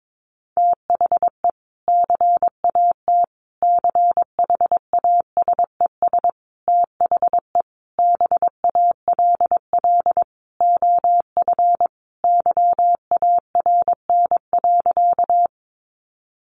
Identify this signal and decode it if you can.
Morse code, heard as a keyed tone, about 22 wpm, 700 Hz. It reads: THE CAT CHASES THE BALL OF YARN.
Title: Additional Easy Phrases 22wpm